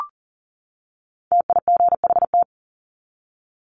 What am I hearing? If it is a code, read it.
Morse code, heard as a keyed tone, about 40 wpm, 700 Hz. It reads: TIGHT
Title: Top 1751-2000 Words 40wpm